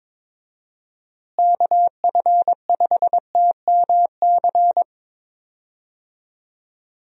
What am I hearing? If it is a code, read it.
Morse code, heard as a keyed tone, about 22 wpm, 700 Hz. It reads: KF5TMC
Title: USA Call Signs - 6 Characters - Repeated 3x - 22wpm